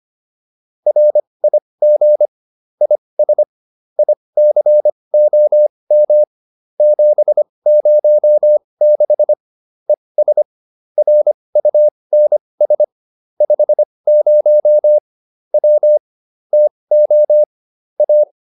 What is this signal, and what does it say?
Morse code, heard as a keyed tone, about 25 wpm, spaced as for 20 wpm, 600 Hz. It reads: RIG IS ICOM 706 ES RUNS 50 W TO A